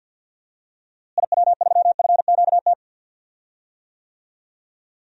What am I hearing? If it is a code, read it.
Morse code, heard as a keyed tone, about 53 wpm, 700 Hz. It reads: IW3FQT
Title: DXCC-100 Call Signs - 6 Characters - Repeated 3x - 50wpm